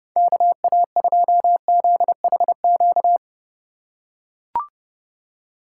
Morse code, heard as a keyed tone, about 31 wpm, 700 Hz. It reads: KA2ZHQ
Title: USA Call Signs - 6 Characters - 4x Character Spacing - 30wpm